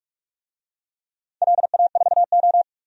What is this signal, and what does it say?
Morse code, heard as a keyed tone, about 44 wpm, 700 Hz. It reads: LA4O
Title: DXCC-100 Call Signs - 4 Characters - 45wpm